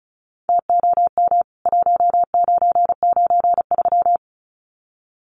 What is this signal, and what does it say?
Morse code, heard as a keyed tone, about 34 wpm, 700 Hz. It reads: TOM 1993
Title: CWT Exchange 35wpm